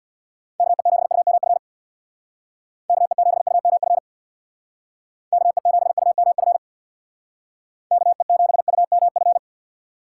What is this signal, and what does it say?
Morse code, heard as a keyed tone, about 66 wpm, 690 Hz. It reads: KE7UMF KE7UMF KE7UMF KE7UMF